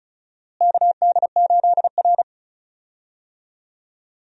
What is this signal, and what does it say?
Morse code, heard as a keyed tone, about 36 wpm, 700 Hz. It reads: KD8R
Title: USA Call Signs - 4 Characters - Repeated 3x - 35wpm